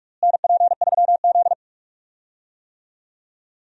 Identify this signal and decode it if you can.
Morse code, heard as a keyed tone, about 46 wpm, 700 Hz. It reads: NP3Z
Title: USA Call Signs - 4 Characters - Repeated 3x - 45wpm